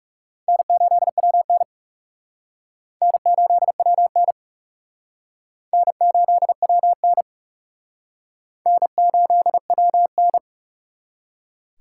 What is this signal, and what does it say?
Morse code, heard as a keyed tone, about 36 wpm, 700 Hz. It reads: N8WN N8WN N8WN N8WN
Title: Speed Racing - USA 1x2 Call Signs 30wpm